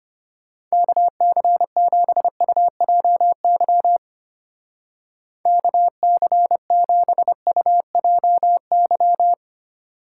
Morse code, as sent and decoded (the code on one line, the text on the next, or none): -.- -.-. --... ..- .--- -.-- / -.- -.-. --... ..- .--- -.--
KC7UJY KC7UJY